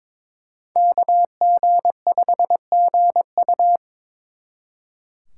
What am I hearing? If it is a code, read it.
Morse code, heard as a keyed tone, about 26 wpm, 700 Hz. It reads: KG5GU
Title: Speed Racing - USA Call Signs 22wpm